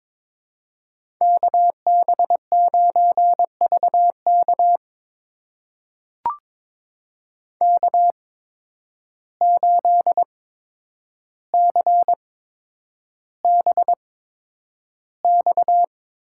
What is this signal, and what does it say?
Morse code, heard as a keyed tone, about 22 wpm, spaced as for 9 wpm, 700 Hz. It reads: KB9VK K8CBX
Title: USA Call Signs - 5 Characters - 8x Character Spacing - 22wpm